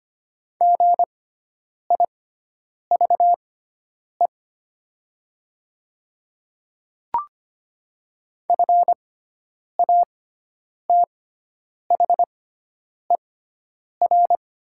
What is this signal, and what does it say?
Morse code, heard as a keyed tone, about 25 wpm, spaced as for 9 wpm, 700 Hz. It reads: GIVE FATHER
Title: 6x Character Spacing - Top 200 Words - 25wpm